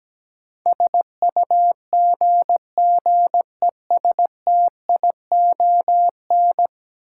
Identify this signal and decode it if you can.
Morse code, heard as a keyed tone, about 17 wpm, 700 Hz. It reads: SUGGESTION
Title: Sentences from Top 1600 Words - New Word Drill Alt - 17wpm